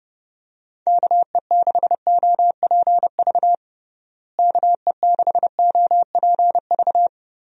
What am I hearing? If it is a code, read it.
Morse code, heard as a keyed tone, about 30 wpm, 700 Hz. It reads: KE6OPV KE6OPV
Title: USA Call Signs - 6 Characters - Repeated 2x - 30wpm